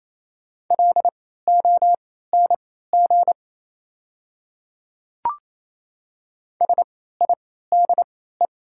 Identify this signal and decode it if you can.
Morse code, heard as a keyed tone, about 28 wpm, spaced as for 16 wpm, 700 Hz. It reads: LONG SIDE
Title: 3x Character Spacing - Top 100 Words - 28wpm